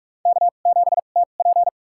Morse code, heard as a keyed tone, about 46 wpm, spaced as for 33 wpm, 690 Hz. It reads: K7TP
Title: USA Call Signs - 4 Characters - 2x Character Spacing - 45wpm